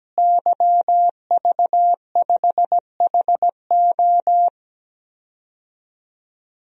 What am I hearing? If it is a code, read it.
Morse code, heard as a keyed tone, about 17 wpm, 700 Hz. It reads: YV5HO
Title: DXCC-100 Call Signs - 5 Characters - Repeated 2x - 17wpm